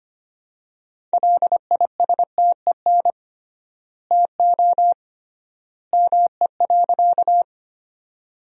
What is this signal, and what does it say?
Morse code, heard as a keyed tone, about 25 wpm, 700 Hz. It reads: LISTEN TO ME.
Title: 3x Word-Spacing - Sentences from Top 400 Words - 25wpm